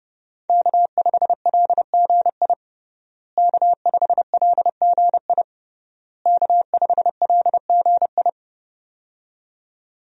Morse code, heard as a keyed tone, about 30 wpm, 700 Hz. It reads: K5LGI K5LGI K5LGI
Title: USA Call Signs - 5 Characters - Repeated 3x - 30wpm